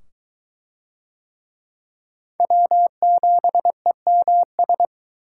Morse code, heard as a keyed tone, about 21 wpm, 700 Hz. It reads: W7EMS